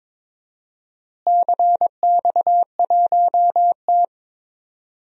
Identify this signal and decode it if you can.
Morse code, heard as a keyed tone, about 22 wpm, 700 Hz. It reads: CX1T
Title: DXCC-100 Call Signs - 4 Characters - Repeated 2x - 22wpm